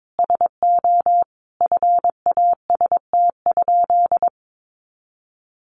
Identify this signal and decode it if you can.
Morse code, heard as a keyed tone, about 22 wpm, 700 Hz. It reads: SO FAST?
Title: Sentences from Top 500 Words 22wpm